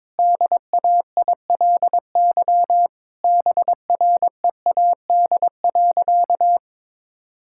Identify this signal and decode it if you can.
Morse code, heard as a keyed tone, about 22 wpm, 700 Hz. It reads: DAILY BREAD.
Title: Sentences from Top 1600 Words 22wpm